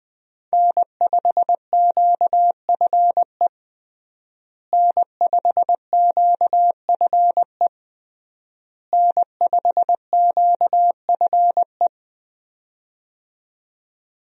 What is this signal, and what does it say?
Morse code, heard as a keyed tone, about 20 wpm, 700 Hz. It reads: N5QFE N5QFE N5QFE
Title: USA Call Signs - 5 Characters - Repeated 3x - 20wpm